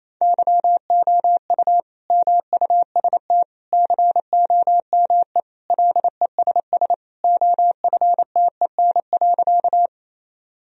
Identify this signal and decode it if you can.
Morse code, heard as a keyed tone, about 27 wpm, 700 Hz. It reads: YOU MUST COME LESS OFTEN.
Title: Sentences from Top 1000 Words - Review Part 2 - 28wpm